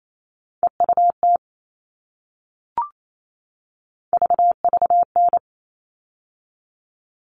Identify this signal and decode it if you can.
Morse code, heard as a keyed tone, about 28 wpm, 700 Hz. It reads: EUT VVN
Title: All Cut Numbers - 3-Digit 28wpm